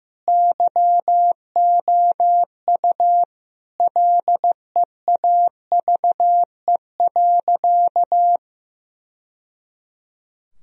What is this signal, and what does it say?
Morse code, heard as a keyed tone, about 15 wpm, 700 Hz. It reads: YOU LEAVE.
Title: Sentences from Top 1000 Words - Review Part 4 - 15wpm